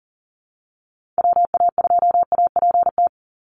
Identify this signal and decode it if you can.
Morse code, heard as a keyed tone, about 40 wpm, 700 Hz. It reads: WA2APT
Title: USA Call Signs 40wpm